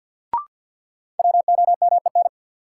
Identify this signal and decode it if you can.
Morse code, heard as a keyed tone, about 47 wpm, 700 Hz. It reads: WOMEN
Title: Sentences from Top 1300 Words - New Word Drill - 50wpm